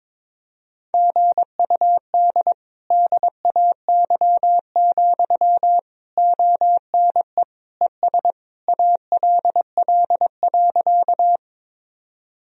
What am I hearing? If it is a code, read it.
Morse code, heard as a keyed tone, about 22 wpm, 700 Hz. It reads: GUD DAY, ONE ES ALL.
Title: Sentences from Top 200 Words and Abbreviations 22wpm